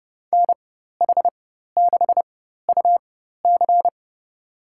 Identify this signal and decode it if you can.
Morse code, heard as a keyed tone, about 30 wpm, spaced as for 14 wpm, 700 Hz. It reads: NH6UC
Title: USA Call Signs - 4x Character Spacing - 30wpm